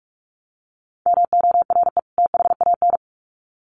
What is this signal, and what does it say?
Morse code, heard as a keyed tone, about 44 wpm, 700 Hz. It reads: MORE THAN
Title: Sets of 2 Words 45wpm